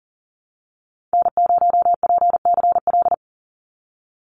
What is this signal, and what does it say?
Morse code, heard as a keyed tone, about 41 wpm, 700 Hz. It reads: N0PCL
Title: NA SOTA Activator Callsigns 40wpm